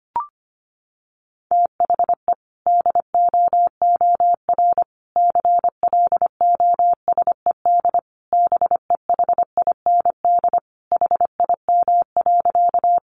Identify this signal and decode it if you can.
Morse code, heard as a keyed tone, about 25 wpm, 700 Hz. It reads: THE DOOR CLOSED BEHIND HIM.
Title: Sentences from Top 1400 Words 25wpm